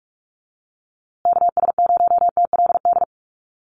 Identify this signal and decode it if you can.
Morse code, heard as a keyed tone, about 46 wpm, 690 Hz. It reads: KS0TLD
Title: USA Call Signs Repeated 3x 45wpm